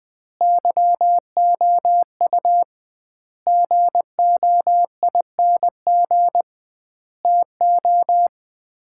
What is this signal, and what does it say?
Morse code, heard as a keyed tone, about 20 wpm, 700 Hz. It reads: YOU GOING TO
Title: 2x Word-Spacing - Sets of 5 Words - 20wpm